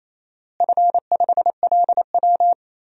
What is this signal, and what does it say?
Morse code, heard as a keyed tone, about 28 wpm, 700 Hz. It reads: F5LW
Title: DXCC-100 Call Signs - 4 Characters - Repeated 3x - 28wpm